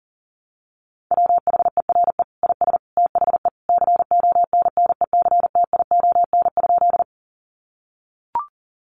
Morse code, heard as a keyed tone, about 37 wpm, 700 Hz. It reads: WHERE IS THE CONNECTION?
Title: Sentences from Top 1300 Words 40wpm